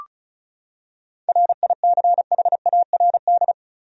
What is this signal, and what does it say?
Morse code, heard as a keyed tone, about 34 wpm, 700 Hz. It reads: RICHARD
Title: USA Common Names - 35wpm